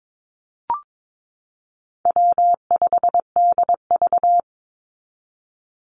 Morse code, heard as a keyed tone, about 22 wpm, 700 Hz. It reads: W5DV
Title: Call Signs Repeated 3x - 22wpm